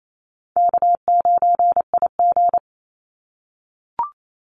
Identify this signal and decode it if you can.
Morse code, heard as a keyed tone, about 28 wpm, 700 Hz. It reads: K9IG
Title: Call Signs - 28wpm